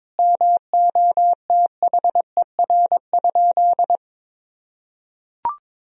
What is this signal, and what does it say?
Morse code, heard as a keyed tone, about 22 wpm, 700 Hz. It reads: MOTHER?
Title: Sentences from Top 1000 Words - Review Part 3 - 22wpm